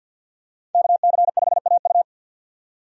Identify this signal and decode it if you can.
Morse code, heard as a keyed tone, about 51 wpm, 700 Hz. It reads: KC5AU
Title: USA Call Signs - 5 Characters - 50wpm